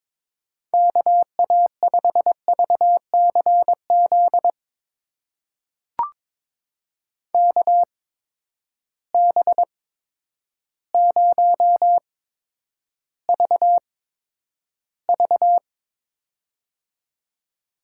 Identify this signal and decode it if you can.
Morse code, heard as a keyed tone, about 22 wpm, spaced as for 10 wpm, 700 Hz. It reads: KA5VCZ KB0VV
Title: USA Call Signs - 8x Character Spacing - 22wpm